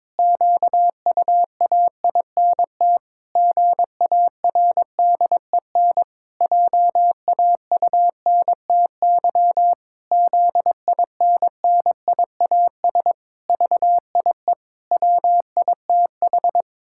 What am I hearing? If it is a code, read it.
Morse code, heard as a keyed tone, about 22 wpm, 700 Hz. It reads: QUAINT GARDEN JAUNTY ZINNIAS VIE WITH
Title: Pangram Challenge v2 - 22wpm